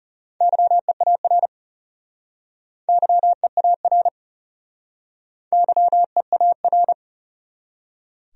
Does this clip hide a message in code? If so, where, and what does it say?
Yes: Morse code from 0.4 to 6.9 s, heard as a keyed tone, about 40 wpm, 700 Hz. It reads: YEAR YEAR YEAR